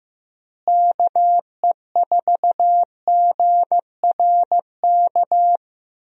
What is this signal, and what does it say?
Morse code, heard as a keyed tone, about 15 wpm, 700 Hz. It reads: KE4GRK
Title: USA Call Signs - 6 Characters - Repeated 2x - 15wpm